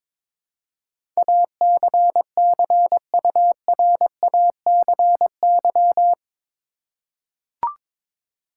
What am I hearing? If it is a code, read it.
Morse code, heard as a keyed tone, about 22 wpm, 700 Hz. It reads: ACCURACY
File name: practice / 22wpm / Eight Letter Words - 22wpm